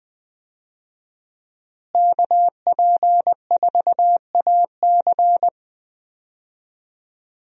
Morse code, heard as a keyed tone, about 20 wpm, 700 Hz. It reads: KP4AC
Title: USA Call Signs - 5 Characters - Repeated 3x - 20wpm